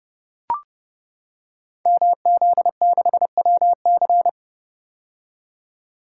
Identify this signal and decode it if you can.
Morse code, heard as a keyed tone, about 30 wpm, 700 Hz. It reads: MZ6WC
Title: Code Groups - 5 Letter:Numbers - 30wpm